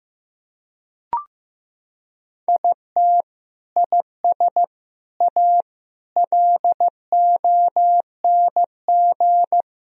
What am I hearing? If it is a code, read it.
Morse code, heard as a keyed tone, about 15 wpm, 700 Hz. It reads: IT IS A LONG
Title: Sentences from Top 1000 Words - Review Part 1 - 15wpm